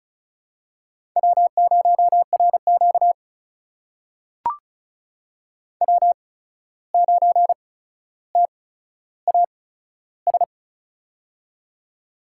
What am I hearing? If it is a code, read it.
Morse code, heard as a keyed tone, about 36 wpm, spaced as for 14 wpm, 700 Hz. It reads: W0RQ W9TAS
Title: USA Call Signs - 8x Character Spacing - 35wpm